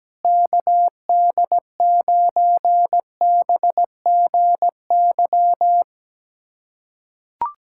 Morse code, heard as a keyed tone, about 17 wpm, 700 Hz. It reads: KD9BGY
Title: USA Callsign DE Callsign - 6 Characters - 17wpm